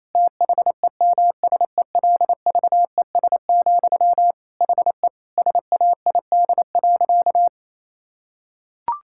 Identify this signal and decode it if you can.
Morse code, heard as a keyed tone, about 27 wpm, 690 Hz. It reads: THEMSELVES, HE SAID.
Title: Sentences from Top 1000 Words - Review Part 6 - 28wpm